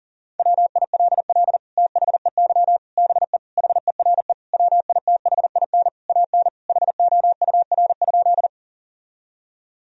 Morse code, heard as a keyed tone, about 40 wpm, 700 Hz. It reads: WILL THEY BE HERE WITHIN AN HOUR?
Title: Sentences from Top 1000 Words - Review Part 4 - 40wpm